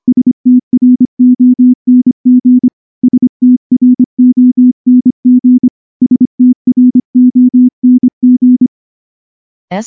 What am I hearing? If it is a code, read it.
Morse code, heard as a keyed tone, about 24 wpm, 250 Hz. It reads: STRONG STRONG STRONG
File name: morse-training-25WPM